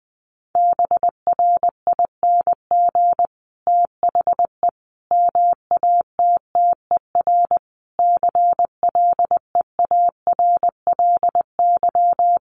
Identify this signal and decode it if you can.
Morse code, heard as a keyed tone, about 20 wpm, 700 Hz. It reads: BRING THE MATTER CLEARLY
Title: Sentences from Top 1300 Words 20wpm